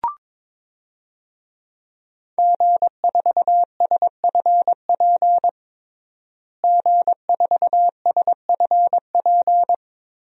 Morse code, heard as a keyed tone, about 22 wpm, 700 Hz. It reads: G4SFP G4SFP
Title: DXCC-100 Call Signs - 5 Characters - Repeated 2x - 22wpm